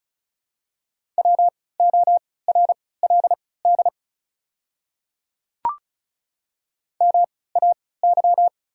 Morse code, heard as a keyed tone, about 36 wpm, spaced as for 20 wpm, 690 Hz. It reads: WORLD MAY
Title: 3x Character Spacing - Top 300 Words - 35wpm